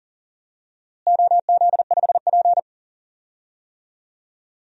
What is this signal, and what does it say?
Morse code, heard as a keyed tone, about 40 wpm, 700 Hz. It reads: OZ5P
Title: DXCC-100 Call Signs - 4 Characters - Repeated 2x - 40wpm